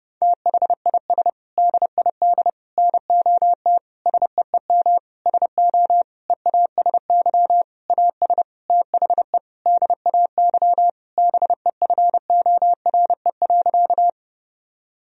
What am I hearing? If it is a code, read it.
Morse code, heard as a keyed tone, about 30 wpm, 700 Hz. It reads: THIS DID NOT SEEM SO EASY AS THE DAY BEFORE.
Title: Sentences from Top 1000 Words - Review Part 1 - 30wpm